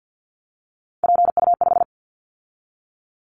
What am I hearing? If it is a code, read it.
Morse code, heard as a keyed tone, about 50 wpm, 700 Hz. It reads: PU5
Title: Callsign Prefixes for Top 100 DXCC - 50wpm